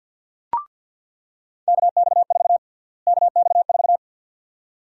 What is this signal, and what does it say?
Morse code, heard as a keyed tone, about 46 wpm, 700 Hz. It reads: KX4 KX4
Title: USA Call Sign Prefixes - Repeated 2x - 50wpm